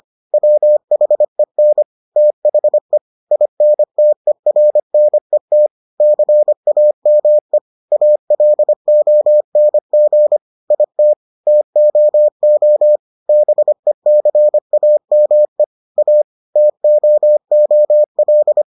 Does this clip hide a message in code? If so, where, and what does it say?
Yes: Morse code from 0.3 to 18.6 s, heard as a keyed tone, about 25 wpm, 600 Hz. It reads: WHEN THE INTERNET CAME ALONG IT TOO BECAME A TOOL